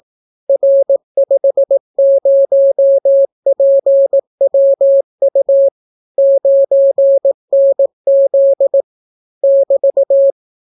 Morse code, heard as a keyed tone, about 18 wpm, 550 Hz. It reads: R50PWU 9NZ <BT>